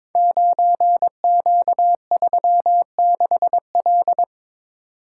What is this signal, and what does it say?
Morse code, heard as a keyed tone, about 22 wpm, 700 Hz. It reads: 9Q36L
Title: Code Groups - 5 Letter:Numbers - 22wpm